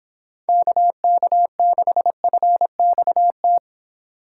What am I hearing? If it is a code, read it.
Morse code, heard as a keyed tone, about 27 wpm, 700 Hz. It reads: KK6FXT
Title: Speed Racing - USA Call Signs 22wpm